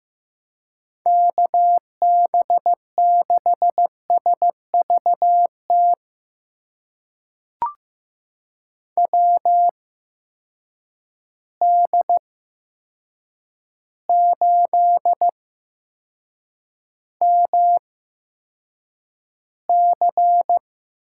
Morse code, heard as a keyed tone, about 16 wpm, spaced as for 7 wpm, 700 Hz. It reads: KB6SVT WD8MC
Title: USA Call Signs - 6 Characters - 8x Character Spacing - 15wpm